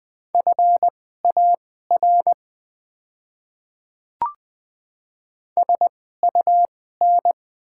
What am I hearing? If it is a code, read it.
Morse code, heard as a keyed tone, about 20 wpm, spaced as for 14 wpm, 700 Hz. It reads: FAR SUN
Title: 2x Character Spacing - Top 300 Words - 20wpm